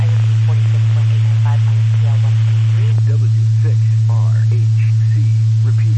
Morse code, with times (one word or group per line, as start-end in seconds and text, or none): none